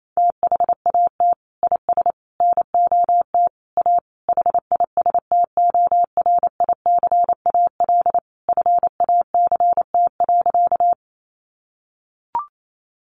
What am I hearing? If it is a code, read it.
Morse code, heard as a keyed tone, about 28 wpm, 700 Hz. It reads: THAT IS NOT A HISTORICAL FACT.
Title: Sentences from Top 600 Words 28wpm